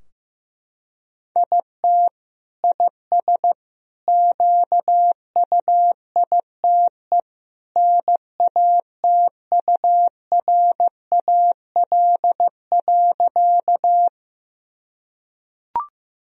Morse code, heard as a keyed tone, about 15 wpm, 700 Hz. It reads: IT IS QUITE NATURAL.
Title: Sentences from Top 1000 Words - Review Part 4 - 15wpm